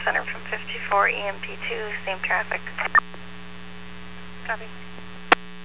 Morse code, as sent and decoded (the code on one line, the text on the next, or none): none